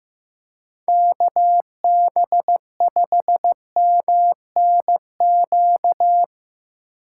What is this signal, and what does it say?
Morse code, heard as a keyed tone, about 17 wpm, 700 Hz. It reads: KB5MNQ
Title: Speed Racing - USA 2x3 Call Signs 15wpm